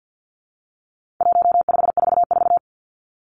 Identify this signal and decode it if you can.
Morse code, heard as a keyed tone, about 50 wpm, 700 Hz. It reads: 1544
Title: Numbers - 4-Digit 50wpm